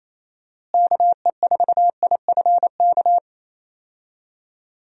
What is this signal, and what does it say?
Morse code, heard as a keyed tone, about 28 wpm, 700 Hz. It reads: KE4IFK
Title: USA Call Signs - 6 Characters - Repeated 3x - 28wpm